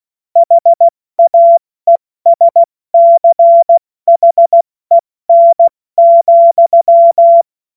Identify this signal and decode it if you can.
Morse code, heard as a keyed tone, about 16 wpm, 680 Hz. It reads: HAESCHEN,
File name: Morsekurs / Witze / Witz 25. Mai 2021, 70 F80